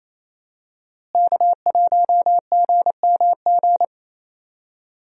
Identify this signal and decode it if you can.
Morse code, heard as a keyed tone, about 28 wpm, 700 Hz. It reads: K1GMG
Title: USA Call Signs - 5 Characters - Repeated 2x - 28wpm